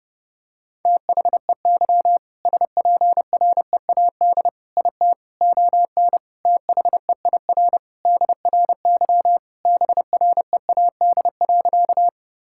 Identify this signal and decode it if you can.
Morse code, heard as a keyed tone, about 32 wpm, 700 Hz. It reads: THEY SPREAD IT ON THEIR DRY BREAD.
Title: Sentences from Top 1600 Words 30wpm